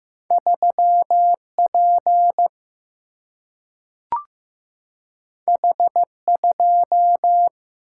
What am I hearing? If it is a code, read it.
Morse code, heard as a keyed tone, about 15 wpm, 700 Hz. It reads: 3P H2
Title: Single-Letter Single-Number 15wpm